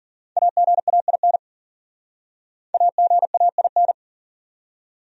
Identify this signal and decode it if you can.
Morse code, heard as a keyed tone, about 45 wpm, 700 Hz. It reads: AGAIN AGAIN
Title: Speed-Racing Top 200 Words 40wpm